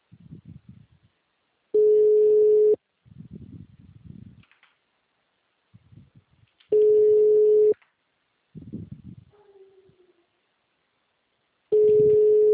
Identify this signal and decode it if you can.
Ringback tone (Europe, 425 Hz).